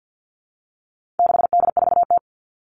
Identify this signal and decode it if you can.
Morse code, heard as a keyed tone, about 52 wpm, 700 Hz. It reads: 6D4T
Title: All Cut Numbers - 4-Digit 50wpm